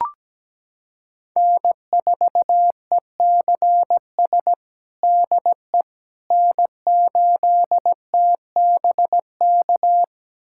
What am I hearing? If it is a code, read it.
Morse code, heard as a keyed tone, about 17 wpm, 700 Hz. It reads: N4ECS DE N8TBK
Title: USA Callsign DE Callsign - 5 Characters - 17wpm